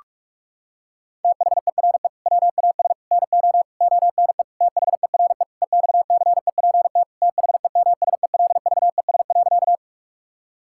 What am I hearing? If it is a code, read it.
Morse code, heard as a keyed tone, about 47 wpm, 700 Hz. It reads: THERE WAS NO ONE THERE EXCEPT THEMSELVES.
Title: Sentences from Top 1600 Words 45wpm